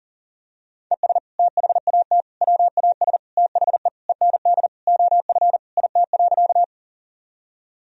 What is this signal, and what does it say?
Morse code, heard as a keyed tone, about 40 wpm, 700 Hz. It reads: ES THAT WAS THE END OF IT.
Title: Sentences from Top 200 Words and Abbreviations 40wpm